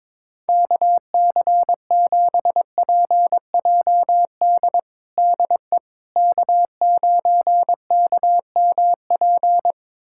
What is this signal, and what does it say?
Morse code, heard as a keyed tone, about 22 wpm, 700 Hz. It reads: KC7PJD DE K9KMP
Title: USA Callsign DE Callsign 22wpm